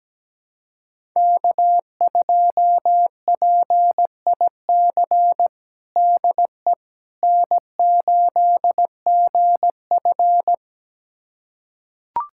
Morse code, heard as a keyed tone, about 17 wpm, 700 Hz. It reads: K2PIC DE N8GF
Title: USA Callsign DE Callsign 17wpm